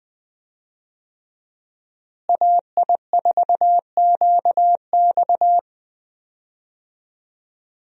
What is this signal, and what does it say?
Morse code, heard as a keyed tone, about 21 wpm, 700 Hz. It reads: AI4QX
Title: USA Call Signs - 5 Characters - Repeated 2x - 20wpm